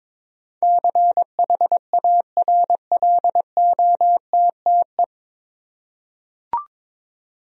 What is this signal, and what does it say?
Morse code, heard as a keyed tone, about 21 wpm, 700 Hz. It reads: CHARLOTTE
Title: USA Common Names - 22wpm